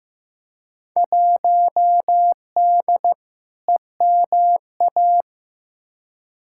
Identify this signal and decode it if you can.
Morse code, heard as a keyed tone, about 15 wpm, 700 Hz. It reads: 1D EMA
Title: Field Day Exchanges - 15wpm